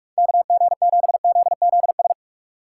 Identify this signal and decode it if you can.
Morse code, heard as a keyed tone, about 46 wpm, 700 Hz. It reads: KG7ZZS
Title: Speed Racing - USA 2x3 Call Signs 45wpm